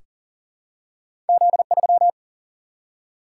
Morse code, heard as a keyed tone, about 40 wpm, 700 Hz. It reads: Z3
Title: Single-Letter Single-Number 40wpm